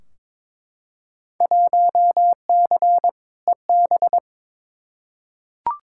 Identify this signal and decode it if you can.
Morse code, heard as a keyed tone, about 22 wpm, 700 Hz. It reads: 1C EB
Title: Field Day Exchanges Encore - 22wpm